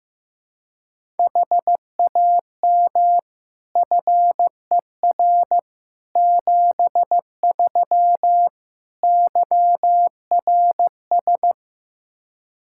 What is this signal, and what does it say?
Morse code, heard as a keyed tone, about 15 wpm, 700 Hz. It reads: HAM FER 73 YRS
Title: Practice QSO - Exchange 3 - More Info - 15wpm